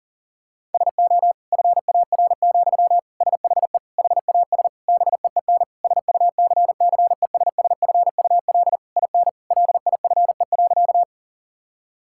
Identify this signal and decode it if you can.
Morse code, heard as a keyed tone, about 37 wpm, 700 Hz. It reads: SO FAR, SHE HAS BEEN SUCCESSFUL IN LIFE.
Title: Sentences from Top 1000 Words - Review Part 6 - 40wpm